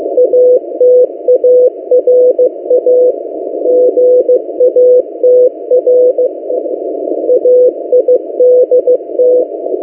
Morse code, heard as a keyed tone, about 15 wpm, 500 Hz. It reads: ATARA GATR AIDT
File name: M08a- -1355utc-8096khz